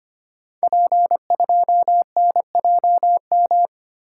Morse code, heard as a keyed tone, about 25 wpm, 700 Hz. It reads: P2NJM
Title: Code Groups - 5 Letter:Numbers - 25wpm